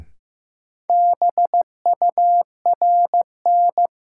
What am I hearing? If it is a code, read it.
Morse code, heard as a keyed tone, about 15 wpm, 700 Hz. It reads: BURN
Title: Sentences from Top 1300 Words - New Word Drill - 15wpm